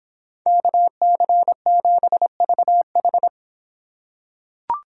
Morse code, heard as a keyed tone, about 26 wpm, 700 Hz. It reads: KC7VH